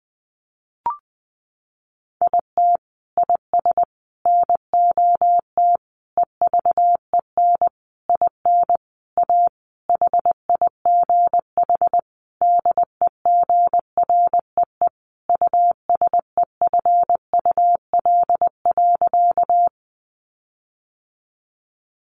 Morse code, heard as a keyed tone, about 19 wpm, 700 Hz. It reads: IT IS NOT EVEN IN A HIGH DEGREE USEFUL.
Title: Sentences from Top 1500 Words 20wpm